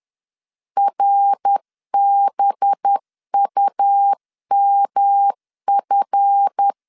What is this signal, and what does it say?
Morse code, heard as a keyed tone, about 10 wpm, 800 Hz. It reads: RBUMF
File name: LEZIONI / 7 GF9Z / 7 GF 10wpm 2514